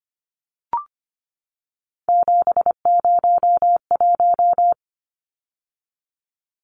Morse code, heard as a keyed tone, about 25 wpm, 700 Hz. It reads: 701
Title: Numbers - 3-Digit 25wpm